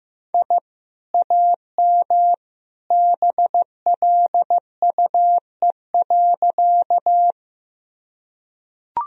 Morse code, heard as a keyed tone, about 15 wpm, 700 Hz. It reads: I AM BLUE.
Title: Sentences from Top 1000 Words - Review Part 3 - 15wpm